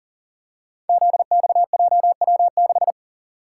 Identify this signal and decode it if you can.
Morse code, heard as a keyed tone, about 42 wpm, 700 Hz. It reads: ZXJW6
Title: Code Groups - 5 Letter:Numbers - 40wpm